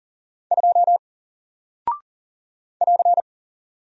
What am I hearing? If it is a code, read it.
Morse code, heard as a keyed tone, about 39 wpm, 700 Hz. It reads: 2 <AR>
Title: Single Character (Letter:Number:Punctuation:Prosign) 40wpm